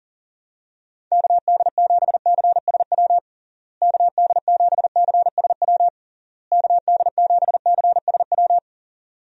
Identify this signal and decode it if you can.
Morse code, heard as a keyed tone, about 37 wpm, 700 Hz. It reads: KD7CSW KD7CSW KD7CSW
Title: USA Call Signs - 6 Characters - Repeated 3x - 40wpm